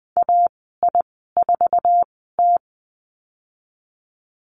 Morse code, heard as a keyed tone, about 20 wpm, spaced as for 15 wpm, 700 Hz. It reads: AI4T
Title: 048 Lesson 9g - Number 4 - Call Signs 20wpm